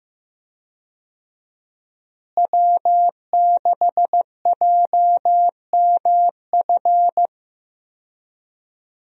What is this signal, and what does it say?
Morse code, heard as a keyed tone, about 15 wpm, 700 Hz. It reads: W6JMF
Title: USA Call Signs - 5 Characters - Repeated 3x - 15wpm